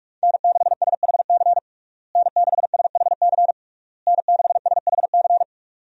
Morse code, heard as a keyed tone, about 46 wpm, 700 Hz. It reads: N6SHC N6SHC N6SHC
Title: USA Call Signs - 5 Characters - Repeated 3x - 45wpm